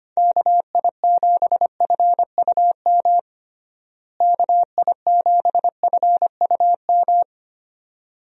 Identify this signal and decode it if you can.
Morse code, heard as a keyed tone, about 25 wpm, 700 Hz. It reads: KI7FUM KI7FUM
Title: USA Call Signs - 6 Characters - Repeated 2x - 25wpm